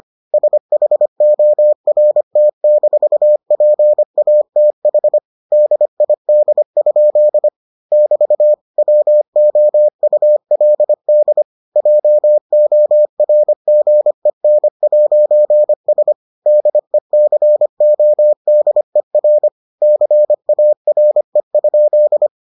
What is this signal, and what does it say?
Morse code, heard as a keyed tone, about 25 wpm, 600 Hz. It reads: SHORT-PATH DID? <BT> WOULD JORGEN'S DECODER CARE?